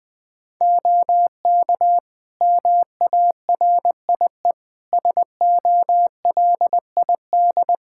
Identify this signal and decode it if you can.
Morse code, heard as a keyed tone, about 20 wpm, 700 Hz. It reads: OK MARIE SOLID
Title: Practice QSO - Exchange 3 - More Info - 20wpm